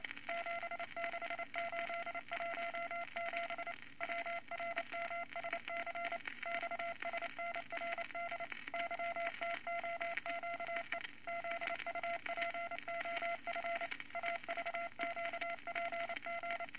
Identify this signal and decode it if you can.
Morse code, heard as a keyed tone, about 28 wpm, 680 Hz, with noise. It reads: 76817 WRMSC XSNRD YTOQE ZUPOF AVQPG
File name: Morse Code Records / 22 A LCM 172 Twenty Words Per Minute